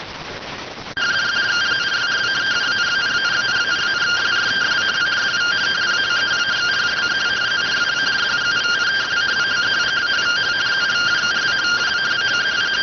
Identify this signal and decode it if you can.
Two-tone teleprinter signal, about 170 Hz shift.